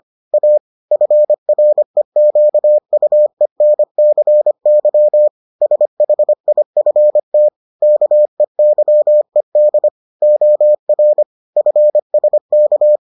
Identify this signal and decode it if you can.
Morse code, heard as a keyed tone, about 25 wpm, 600 Hz. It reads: A FREQUENCY SHIFT KEYED OR FSK